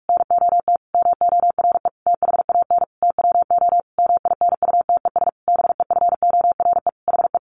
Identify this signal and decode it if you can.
Morse code, heard as a keyed tone, about 44 wpm, 700 Hz. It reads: NOT MORE THAN TWO MINUTES BEFORE HE
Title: A Sound Of Thunder 45wpm